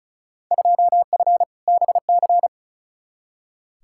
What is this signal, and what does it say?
Morse code, heard as a keyed tone, about 36 wpm, 700 Hz. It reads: 2F BC
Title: Field Day Exchanges Encore - 35wpm